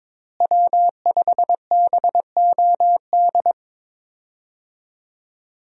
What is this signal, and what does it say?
Morse code, heard as a keyed tone, about 22 wpm, 690 Hz. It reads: W5BOD
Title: USA Call Signs - 5 Characters - Repeated 3x - 22wpm